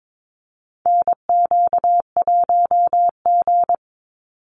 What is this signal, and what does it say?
Morse code, heard as a keyed tone, about 22 wpm, 700 Hz. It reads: NQ1G
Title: USA Call Signs 22wpm